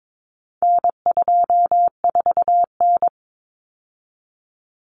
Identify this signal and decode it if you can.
Morse code, heard as a keyed tone, about 22 wpm, 700 Hz. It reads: N24N
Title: Common Cut Numbers - 4-Digit 22wpm